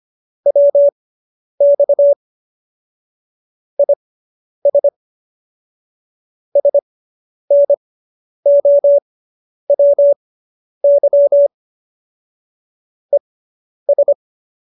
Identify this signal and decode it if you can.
Morse code, heard as a keyed tone, about 25 wpm, spaced as for 10 wpm, 570 Hz. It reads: WX IS SNOWY ES
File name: QSO220 10